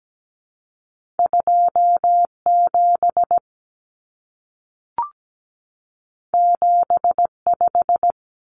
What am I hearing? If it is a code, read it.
Morse code, heard as a keyed tone, about 17 wpm, 700 Hz. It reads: 27 75
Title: Common Cut Numbers - 2-Digit 17wpm